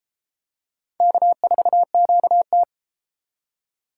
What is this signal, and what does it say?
Morse code, heard as a keyed tone, about 31 wpm, 700 Hz. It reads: K4QT